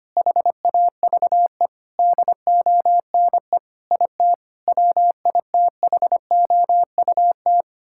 Morse code, heard as a keyed tone, about 25 wpm, 700 Hz. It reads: HAVE DONE IT WITHOUT
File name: practice / 25wpm / Difficult Sentences with Hints - 25wpm